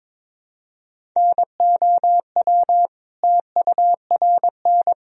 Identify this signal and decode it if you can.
Morse code, heard as a keyed tone, about 22 wpm, 700 Hz. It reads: NOW TURN
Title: Sentences from Top 1000 Words - Review Part 1 - 22wpm